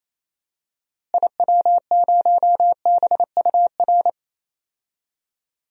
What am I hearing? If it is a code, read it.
Morse code, heard as a keyed tone, about 28 wpm, 700 Hz. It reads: IW0BUR
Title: DXCC-100 Call Signs - 6 Characters - Repeated 3x - 28wpm